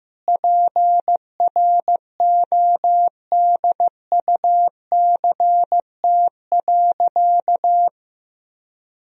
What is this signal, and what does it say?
Morse code, heard as a keyed tone, about 15 wpm, 700 Hz. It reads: PRODUCT.
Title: Sentences from Top 1000 Words - Review Part 3 - 15wpm